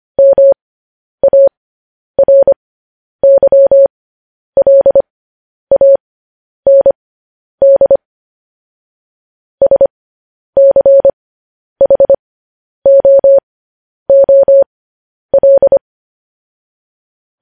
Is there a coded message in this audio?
Yes: Morse code from 0.2 to 15.8 s, heard as a keyed tone, about 25 wpm, spaced as for 10 wpm, 550 Hz. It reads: MARYLAND SCHOOL